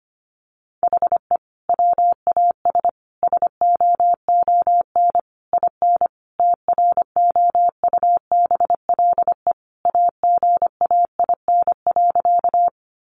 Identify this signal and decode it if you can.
Morse code, heard as a keyed tone, about 25 wpm, 700 Hz. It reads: HE WAS SOON IN TROUBLE AGAIN.
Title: Sentences from Top 1200 Words 25wpm